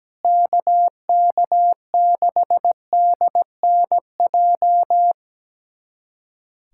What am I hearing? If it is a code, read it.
Morse code, heard as a keyed tone, about 17 wpm, 700 Hz. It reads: KK6DNJ
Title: USA Callsign DE Callsign - 6 Characters - 17wpm